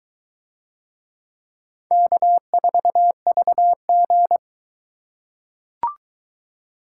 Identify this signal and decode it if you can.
Morse code, heard as a keyed tone, about 22 wpm, 700 Hz. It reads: K4VG